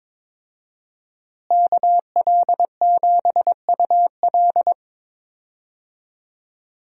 Morse code, heard as a keyed tone, about 22 wpm, 700 Hz. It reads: KL7UL
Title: USA Call Signs - 5 Characters - Repeated 3x - 22wpm